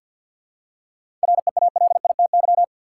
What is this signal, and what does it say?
Morse code, heard as a keyed tone, about 51 wpm, 700 Hz. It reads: REALITY
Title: Seven Letter Words - 50wpm